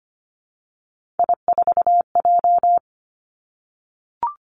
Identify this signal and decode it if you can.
Morse code, heard as a keyed tone, about 25 wpm, 700 Hz. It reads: I4J
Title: Three Challenging Letter-Number Combinations 25wpm